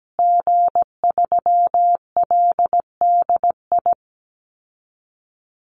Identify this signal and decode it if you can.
Morse code, heard as a keyed tone, about 17 wpm, 700 Hz. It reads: G3LDI
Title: Call Signs Repeated 2x - 17wpm